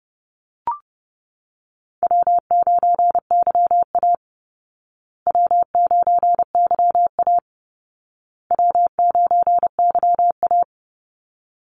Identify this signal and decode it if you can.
Morse code, heard as a keyed tone, about 30 wpm, 700 Hz. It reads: W9YA W9YA W9YA
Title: Easy Call Signs Repeated 3x - 30wpm